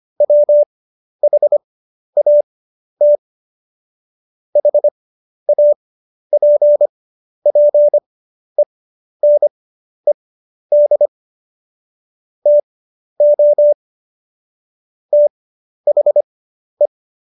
Morse code, heard as a keyed tone, about 25 wpm, spaced as for 11 wpm, 590 Hz. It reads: WHAT HAPPENED TO THE
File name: Jokes-25-char-10-eff